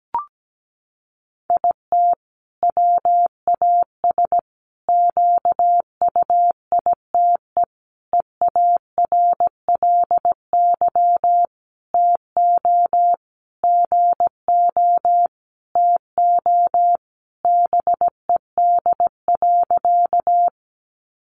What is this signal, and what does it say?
Morse code, heard as a keyed tone, about 17 wpm, 700 Hz. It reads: IT WAS QUITE EARLY TO GO TO BED.
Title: Sentences from Top 700 Words 17wpm